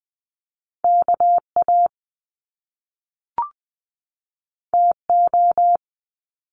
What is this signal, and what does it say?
Morse code, heard as a keyed tone, about 20 wpm, 700 Hz. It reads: KA TO
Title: Top Two Letter Combinations in Words 20wpm